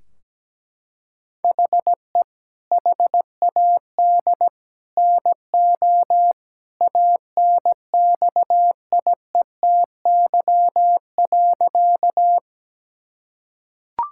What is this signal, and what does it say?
Morse code, heard as a keyed tone, about 17 wpm, 700 Hz. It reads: HE HAD NO ANXIETY.
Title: Sentences from Top 1600 Words 17wpm